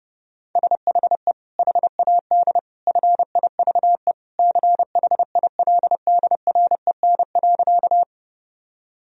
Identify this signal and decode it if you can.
Morse code, heard as a keyed tone, about 32 wpm, 700 Hz. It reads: SHE HAD FIVE CHILDREN.
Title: Sentences from Top 1000 Words - Review Part 2 - 30wpm